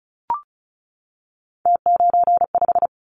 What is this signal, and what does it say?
Morse code, heard as a keyed tone, about 36 wpm, 700 Hz. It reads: T95
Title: Callsign Prefixes for Top 100 DXCC - 35wpm